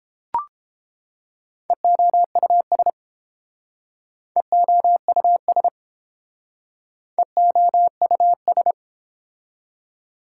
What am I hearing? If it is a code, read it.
Morse code, heard as a keyed tone, about 29 wpm, 700 Hz. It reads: EOUS EOUS EOUS